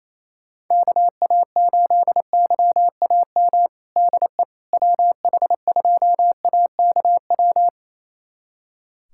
Morse code, heard as a keyed tone, about 28 wpm, 700 Hz. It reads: KA8YAM DE WH2AKW
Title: USA Callsign DE Callsign - 6 Characters - 28wpm